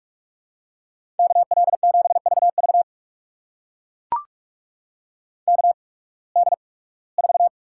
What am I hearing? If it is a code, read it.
Morse code, heard as a keyed tone, about 44 wpm, spaced as for 23 wpm, 700 Hz. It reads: KL7VV KD4
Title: USA Call Signs - 8x Character Spacing - 45wpm